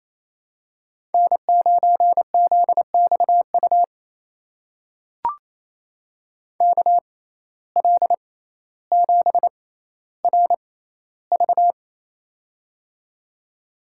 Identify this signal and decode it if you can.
Morse code, heard as a keyed tone, about 28 wpm, spaced as for 14 wpm, 700 Hz. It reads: N9ZXU KL7RV
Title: USA Call Signs - 5 Characters - 6x Character Spacing - 28wpm